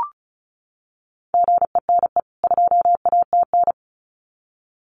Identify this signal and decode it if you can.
Morse code, heard as a keyed tone, about 34 wpm, 700 Hz. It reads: GENE 2ATN
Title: CWT Exchange 35wpm